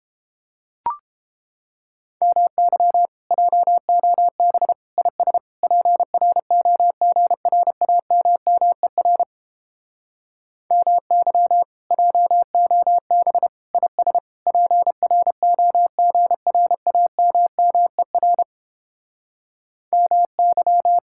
Morse code, heard as a keyed tone, about 29 wpm, 700 Hz. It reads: MY JOB IS PROGRAMMER MY JOB IS PROGRAMMER MY